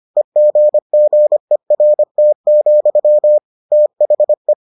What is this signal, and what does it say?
Morse code, heard as a keyed tone, about 25 wpm, 600 Hz. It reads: EGGERT, THE